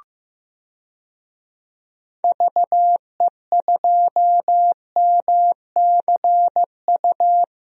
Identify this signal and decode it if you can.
Morse code, heard as a keyed tone, about 15 wpm, 700 Hz. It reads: VE2MCU
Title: DXCC-100 Call Signs - 6 Characters - Repeated 3x - 15wpm